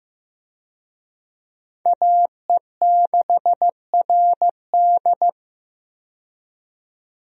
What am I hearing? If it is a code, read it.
Morse code, heard as a keyed tone, about 15 wpm, 700 Hz. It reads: AE6RD
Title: USA Call Signs - 5 Characters - 15wpm